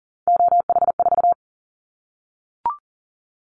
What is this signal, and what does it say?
Morse code, heard as a keyed tone, about 40 wpm, 700 Hz. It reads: OH4
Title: Callsign Prefixes for Top 25 DXCC - 40wpm